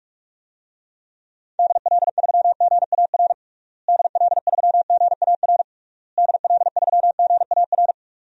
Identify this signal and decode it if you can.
Morse code, heard as a keyed tone, about 46 wpm, 700 Hz. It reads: DL3GAR DL3GAR DL3GAR
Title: DXCC-100 Call Signs - 6 Characters - Repeated 3x - 45wpm